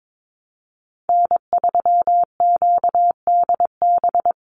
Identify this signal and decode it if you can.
Morse code, heard as a keyed tone, about 22 wpm, 700 Hz. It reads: N3QDB
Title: USA Call Signs 22wpm